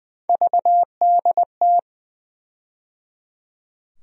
Morse code, heard as a keyed tone, about 20 wpm, 700 Hz. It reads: VDT
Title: USA Call Sign Postfixes - 3 Letters 20wpm